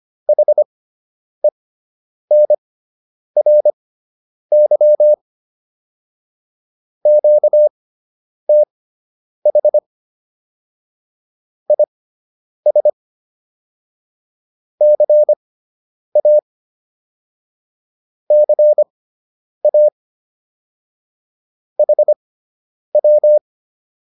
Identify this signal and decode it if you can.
Morse code, heard as a keyed tone, about 25 wpm, spaced as for 9 wpm, 600 Hz. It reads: HENRY QTH IS CA CA HW